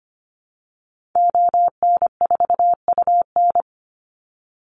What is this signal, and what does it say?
Morse code, heard as a keyed tone, about 25 wpm, 700 Hz. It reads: ON4UN
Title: Call Signs - 25wpm